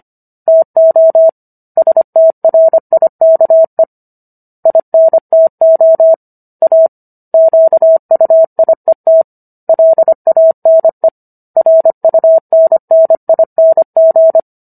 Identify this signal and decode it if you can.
Morse code, heard as a keyed tone, about 25 wpm, 650 Hz. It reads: TO STRIKE INTO A QUIET LANE RUNNING